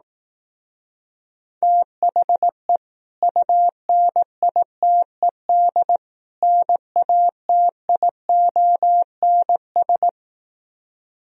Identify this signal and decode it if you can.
Morse code, heard as a keyed tone, about 18 wpm, 700 Hz. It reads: THE UNITED NATIONS
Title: Speed-Racing - Sets of 3 Words 15wpm